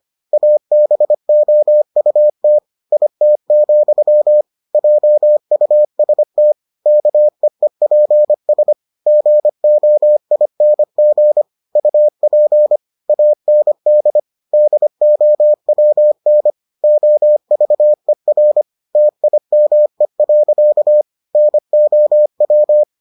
Morse code, heard as a keyed tone, about 25 wpm, 600 Hz. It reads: ABOUT IT, JUST KEEPS GOING UP AND DOWN OVER TIME. NOW